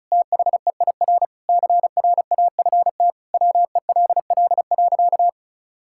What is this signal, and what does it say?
Morse code, heard as a keyed tone, about 34 wpm, 700 Hz. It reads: THEIR CRAFT WELL.
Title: Sentences from Top 1000 Words - Review Part 6 - 35wpm